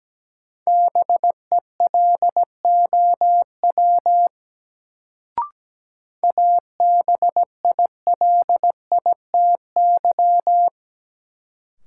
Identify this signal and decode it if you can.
Morse code, heard as a keyed tone, about 17 wpm, 700 Hz. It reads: BELOW ABILITY
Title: Sentences from Top 1200 Words - New Word Drill Alt - 17wpm